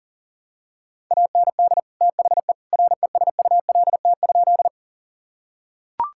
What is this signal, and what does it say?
Morse code, heard as a keyed tone, about 40 wpm, 700 Hz. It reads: AND THE RESULT?
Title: Sentences from Top 1000 Words - Review Part 5 - 40wpm